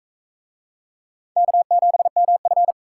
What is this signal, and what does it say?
Morse code, heard as a keyed tone, about 40 wpm, 700 Hz. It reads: K7MF